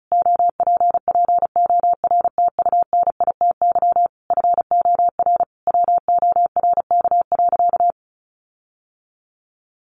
Morse code, heard as a keyed tone, about 34 wpm, 700 Hz. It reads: OPPORTUNITY FOR WORK.
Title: Sentences from Top 1400 Words 35wpm